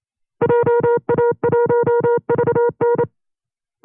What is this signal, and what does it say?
Morse code, heard as a keyed tone, about 28 wpm, 480 Hz. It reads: JA1VN
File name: FreeFiles / CW-practice / 25WPM-side1